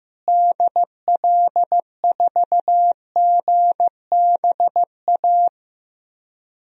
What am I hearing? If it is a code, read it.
Morse code, heard as a keyed tone, about 15 wpm, 700 Hz. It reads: DL4GBA
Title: DXCC-100 Call Signs - 6 Characters - Repeated 3x - 15wpm